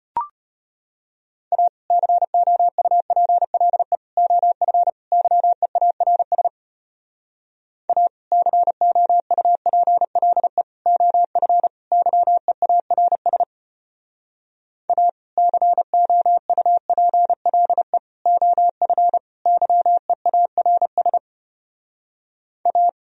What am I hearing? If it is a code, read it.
Morse code, heard as a keyed tone, about 33 wpm, 700 Hz. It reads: A COUPLE OF YEARS A COUPLE OF YEARS A COUPLE OF YEARS A